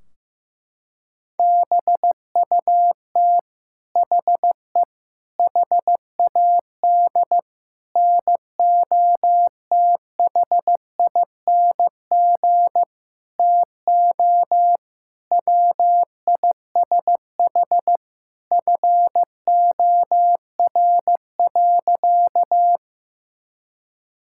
Morse code, heard as a keyed tone, about 15 wpm, 700 Hz. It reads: BUT HE HAD NOTHING TO WISH FOR.
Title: Sentences from Top 1600 Words 15wpm